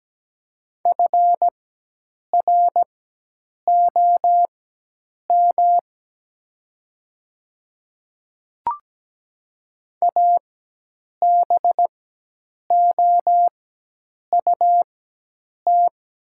Morse code, heard as a keyed tone, about 17 wpm, spaced as for 8 wpm, 700 Hz. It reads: FROM ABOUT
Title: 4x Character Spacing - Top 100 Words - 17wpm